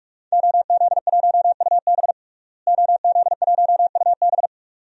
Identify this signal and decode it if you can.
Morse code, heard as a keyed tone, about 47 wpm, 700 Hz. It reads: OZ1UB OZ1UB
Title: DXCC-100 Call Signs - 5 Characters - Repeated 2x - 45wpm